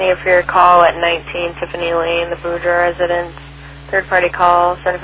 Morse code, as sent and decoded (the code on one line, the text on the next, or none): none